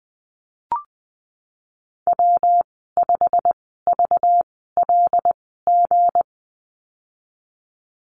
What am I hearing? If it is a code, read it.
Morse code, heard as a keyed tone, about 19 wpm, spaced as for 14 wpm, 700 Hz. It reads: W5VLG
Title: 193 Lesson 31g - Letter v - Call Signs 20wpm